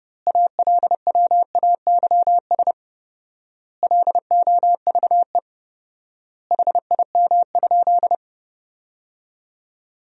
Morse code, heard as a keyed tone, about 30 wpm, 700 Hz. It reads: ALWAYS LOVE HIM?
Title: 4x Word-Spacing - Sentences from Top 400 Words - 30wpm